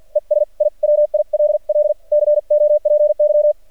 Morse code, heard as a keyed tone, about 46 wpm, 600 Hz. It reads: ES I WILL COPY
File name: archived / CRSnet